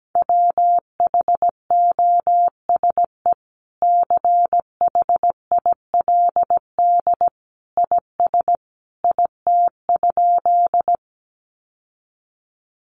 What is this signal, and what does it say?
Morse code, heard as a keyed tone, about 17 wpm, 700 Hz. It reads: WHOSE CHILD IS IT?
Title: Sentences from Top 1200 Words 17wpm